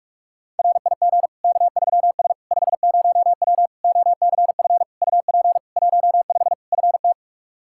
Morse code, heard as a keyed tone, about 46 wpm, 700 Hz. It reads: RIG K3S 50W OCF UP 15 FT
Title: Practice QSO - Exchange 2 - Info - 45wpm